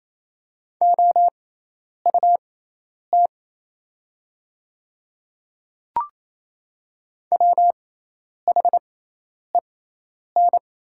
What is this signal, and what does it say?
Morse code, heard as a keyed tone, about 28 wpm, spaced as for 10 wpm, 700 Hz. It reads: OUT WHEN
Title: 6x Character Spacing - Top 200 Words - 28wpm